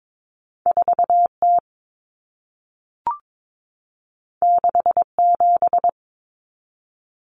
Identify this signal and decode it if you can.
Morse code, heard as a keyed tone, about 22 wpm, 700 Hz. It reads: 4T 67
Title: Common Cut Numbers - 2-Digit 22wpm